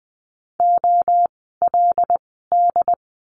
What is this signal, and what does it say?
Morse code, heard as a keyed tone, about 20 wpm, spaced as for 15 wpm, 700 Hz. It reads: OLD
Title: 146 Lesson 24f - Letter y - Words 20wpm